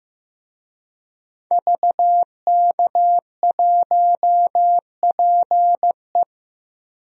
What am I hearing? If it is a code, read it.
Morse code, heard as a keyed tone, about 16 wpm, 700 Hz. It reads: VK1PE
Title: DXCC-100 Call Signs - 5 Characters - Repeated 3x - 15wpm